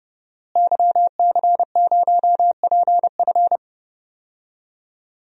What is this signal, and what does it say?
Morse code, heard as a keyed tone, about 29 wpm, 700 Hz. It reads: YC0PF
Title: DXCC-100 Call Signs - 5 Characters - Repeated 2x - 30wpm